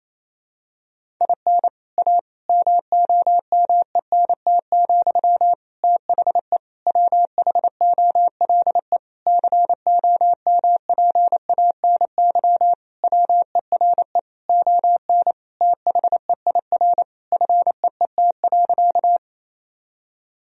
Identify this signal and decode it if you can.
Morse code, heard as a keyed tone, about 28 wpm, 700 Hz. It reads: IN A MOMENT, THE WHOLE COMPANY WERE ON THEIR FEET.
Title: Sentences from Top 1000 Words - Review Part 3 - 28wpm